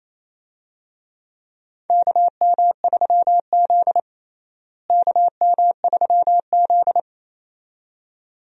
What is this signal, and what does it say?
Morse code, heard as a keyed tone, about 28 wpm, 700 Hz. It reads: KM3Z KM3Z
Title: USA Call Signs - 4 Characters - Repeated 2x - 28wpm